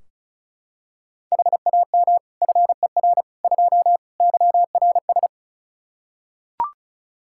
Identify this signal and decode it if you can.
Morse code, heard as a keyed tone, about 34 wpm, 700 Hz. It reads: HAM FER 2 YRS
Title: Practice QSO - Exchange 3 - More Info - 35wpm